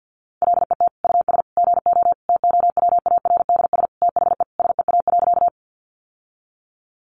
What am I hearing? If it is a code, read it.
Morse code, heard as a keyed tone, about 48 wpm, 700 Hz. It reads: LET US GO TOWARDS THE SEA.
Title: Sentences from Top 1100 Words 50wpm